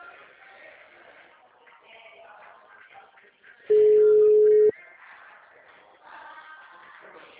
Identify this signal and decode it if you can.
Ringback tone (Europe, 425 Hz).